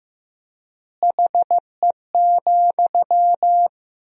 Morse code, heard as a keyed tone, about 15 wpm, 700 Hz. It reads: HE,
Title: 3x Word-Spacing - Sentences from Top 500 Words - 15wpm